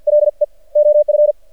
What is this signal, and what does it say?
Morse code, heard as a keyed tone, about 48 wpm, 600 Hz. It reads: LE OF
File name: archived / CRSnet